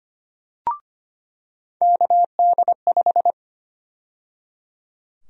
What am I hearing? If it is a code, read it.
Morse code, heard as a keyed tone, about 25 wpm, 700 Hz. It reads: KD5
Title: USA Call Sign Prefixes - 25wpm